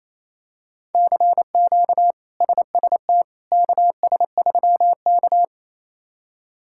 Morse code, heard as a keyed tone, about 28 wpm, 700 Hz. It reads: CQ SST KS3K
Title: K1USN SST QSO Practice - 28wpm